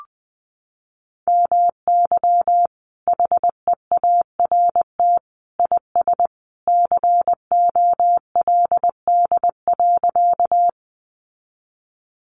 Morse code, heard as a keyed tone, about 20 wpm, 700 Hz. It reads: MY HEART IS COLD.
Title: Sentences from Top 1200 Words 20wpm